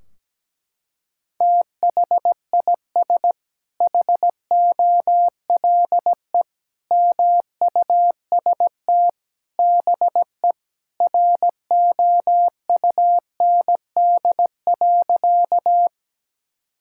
Morse code, heard as a keyed tone, about 17 wpm, 700 Hz. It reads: THIS HOLE MUST BE ROUND.
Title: Sentences from Top 1600 Words 17wpm